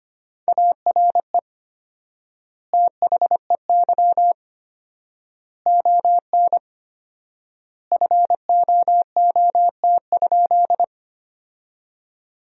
Morse code, heard as a keyed tone, about 25 wpm, 700 Hz. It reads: ARE THEY ON FOOT?
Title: 4x Word-Spacing - Sentences from Top 500 Words - 25wpm